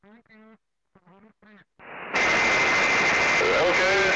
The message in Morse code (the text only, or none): none